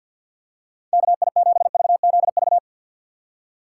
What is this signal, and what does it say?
Morse code, heard as a keyed tone, about 53 wpm, 700 Hz. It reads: KI7VZV
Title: USA Call Signs - 6 Characters - 6x Character Spacing - 50wpm